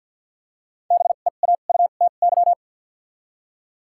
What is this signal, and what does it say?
Morse code, heard as a keyed tone, about 53 wpm, spaced as for 37 wpm, 700 Hz. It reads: BEAUTY
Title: 2x Character Spacing - Top 500 Words - 50wpm